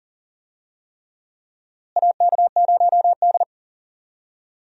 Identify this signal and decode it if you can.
Morse code, heard as a keyed tone, about 40 wpm, 700 Hz. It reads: AK0D
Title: USA Call Signs - 4 Characters - Repeated 3x - 40wpm